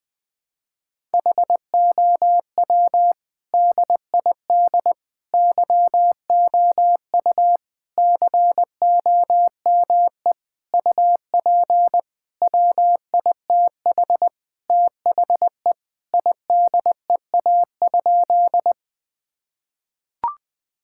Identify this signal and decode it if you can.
Morse code, heard as a keyed tone, about 21 wpm, 700 Hz. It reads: HOW DID YOU COME UP WITH THE IDEA?
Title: Sentences with 8-Word Phrases - 20wpm